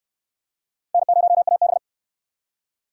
Morse code, heard as a keyed tone, about 58 wpm, 700 Hz. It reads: N0AB